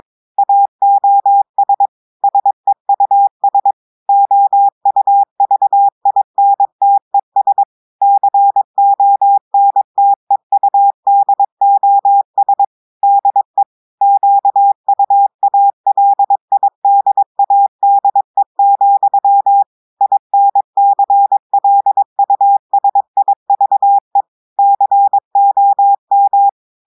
Morse code, heard as a keyed tone, about 22 wpm, 800 Hz. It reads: AOS SEUS OUVINTES CONTEUDOS DE QUALIDADE, INCLUSIVE COM
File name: audios / portugais / portugais texte en clair 2 vitesse 22 mots minutes